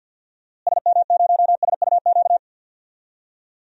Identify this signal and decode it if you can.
Morse code, heard as a keyed tone, about 46 wpm, 700 Hz. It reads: SM0SUQ
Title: DXCC-100 Call Signs - 6 Characters - Repeated 3x - 50wpm